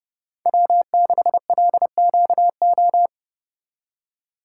Morse code, heard as a keyed tone, about 37 wpm, 700 Hz. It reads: W6LQO
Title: Speed Racing - USA Call Signs 30wpm